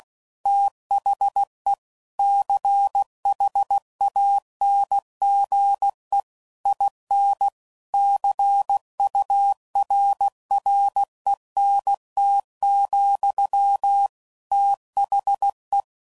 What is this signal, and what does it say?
Morse code, heard as a keyed tone, about 16 wpm, 800 Hz. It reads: THE CHANGE IN CURRENT, THE
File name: CW Code Groups @15WPM/Technical Subjects